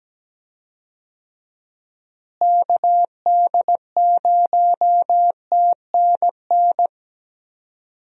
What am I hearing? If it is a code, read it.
Morse code, heard as a keyed tone, about 17 wpm, 700 Hz. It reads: KD0TNN
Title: USA Call Signs - 6 Characters - Repeated 3x - 17wpm